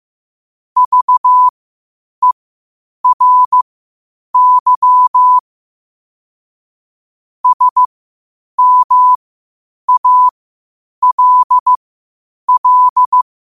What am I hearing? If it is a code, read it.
Morse code, heard as a keyed tone, about 15 wpm, spaced as for 8 wpm, 1000 Hz. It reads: VERY SMALL
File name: qotdfiles / 8wpm